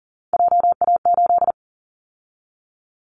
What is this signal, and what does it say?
Morse code, heard as a keyed tone, about 40 wpm, 700 Hz. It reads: JA8
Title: Callsign Prefixes for Top 25 DXCC - 40wpm